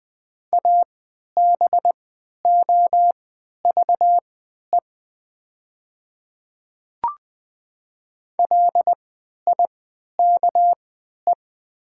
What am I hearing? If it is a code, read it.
Morse code, heard as a keyed tone, about 20 wpm, spaced as for 11 wpm, 700 Hz. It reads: ABOVE LIKE
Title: 3x Character Spacing - Top 500 Words - 20wpm